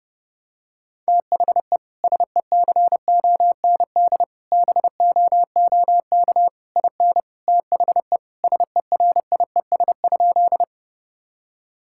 Morse code, heard as a keyed tone, about 30 wpm, 700 Hz. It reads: THE SECOND BOOK IN THE SERIES?
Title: Sentences from Top 1000 Words - Review Part 5 - 30wpm